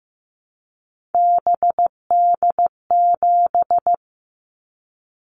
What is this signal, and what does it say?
Morse code, heard as a keyed tone, about 15 wpm, 700 Hz. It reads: BD7
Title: Callsign Prefixes for Top 100 DXCC - 15wpm